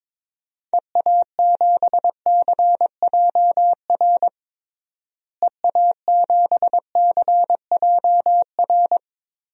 Morse code, heard as a keyed tone, about 22 wpm, 700 Hz. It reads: EA7CJR EA7CJR
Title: DXCC-100 Call Signs - 6 Characters - Repeated 3x - 22wpm